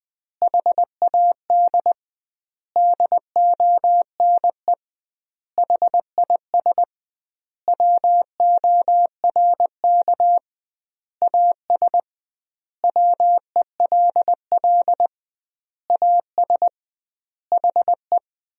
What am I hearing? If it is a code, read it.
Morse code, heard as a keyed tone, about 21 wpm, 700 Hz. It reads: HAD DONE HIS WORK AS WELL AS HE
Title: 2x Word-Spacing - Sentences from Top 500 Words - 20wpm